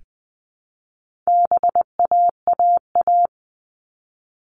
Morse code, heard as a keyed tone, about 20 wpm, 700 Hz. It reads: BAAA
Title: All Cut Numbers - 4-Digit 20wpm